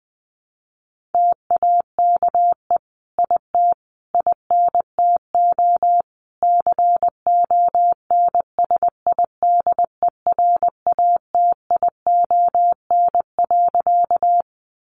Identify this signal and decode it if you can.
Morse code, heard as a keyed tone, about 20 wpm, 700 Hz. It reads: TAKE IT INTO CONSIDERATION.
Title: Sentences from Top 1500 Words 20wpm